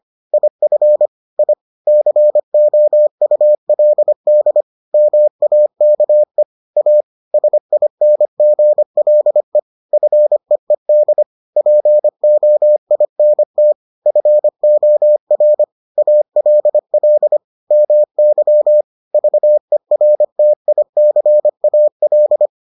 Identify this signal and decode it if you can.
Morse code, heard as a keyed tone, about 25 wpm, 600 Hz. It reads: IF I COULD MAKE A SINGLE FEED POINT FOR ALL MY VERTICAL